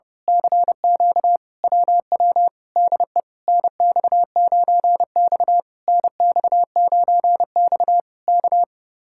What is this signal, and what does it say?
Morse code, heard as a keyed tone, about 30 wpm, 700 Hz. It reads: CQ WW DE NX9X NX9X K
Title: Speed-Racing Common QSO Phrases Encore - 30wpm